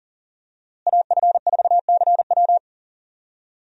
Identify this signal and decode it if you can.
Morse code, heard as a keyed tone, about 40 wpm, 700 Hz. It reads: AF4CW
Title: USA Call Signs - 5 Characters - Repeated 2x - 40wpm